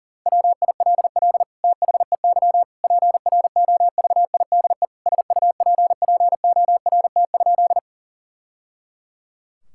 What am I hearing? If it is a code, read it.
Morse code, heard as a keyed tone, about 37 wpm, 700 Hz. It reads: WILL THEY PROVIDE SUPPORT?
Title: Sentences from Top 1000 Words - Review Part 5 - 40wpm